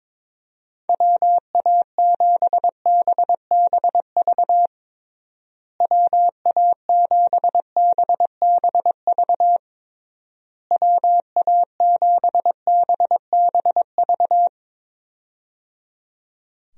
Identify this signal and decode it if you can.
Morse code, heard as a keyed tone, about 22 wpm, 700 Hz. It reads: WA7BBV WA7BBV WA7BBV
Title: USA Call Signs - 6 Characters - Repeated 3x - 22wpm